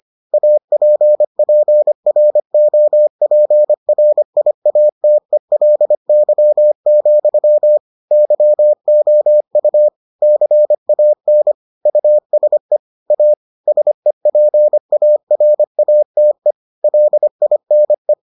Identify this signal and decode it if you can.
Morse code, heard as a keyed tone, about 25 wpm, 600 Hz. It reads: APPROPRIATELY, YOU CAN USE A SEPARATE LINE